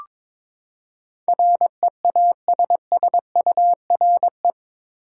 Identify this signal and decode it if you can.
Morse code, heard as a keyed tone, about 22 wpm, 700 Hz. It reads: REASSURE
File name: practice / 22wpm / Eight Letter Words - 22wpm